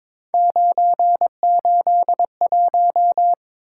Morse code, heard as a keyed tone, about 22 wpm, 700 Hz. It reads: 981
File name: practice / 22wpm / Three Letter-Number Combinations - 22wpm